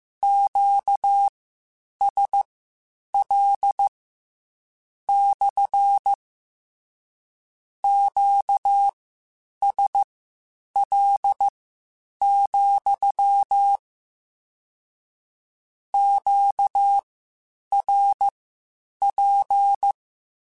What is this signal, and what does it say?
Morse code, heard as a keyed tone, about 15 wpm, spaced as for 8 wpm, 800 Hz. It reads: QSL / QSL, QRP
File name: Q Calls / Q Calls-mp3s / Q CALLS SPSL @13